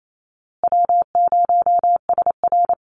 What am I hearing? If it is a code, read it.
Morse code, heard as a keyed tone, about 28 wpm, 700 Hz. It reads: W0SR
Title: USA Call Signs Repeated 3x 28wpm